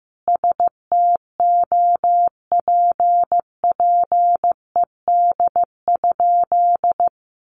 Morse code, heard as a keyed tone, about 15 wpm, 700 Hz. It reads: STOPPED?
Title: A Sound Of Thunder 15wpm